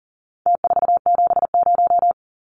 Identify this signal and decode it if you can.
Morse code, heard as a keyed tone, about 40 wpm, 700 Hz. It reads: T470
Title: Numbers - Contest Style - 4-Digit 40wpm